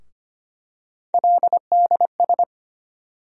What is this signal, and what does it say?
Morse code, heard as a keyed tone, about 25 wpm, 700 Hz. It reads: LDS
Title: USA Call Sign Postfixes - 3 Letters 25wpm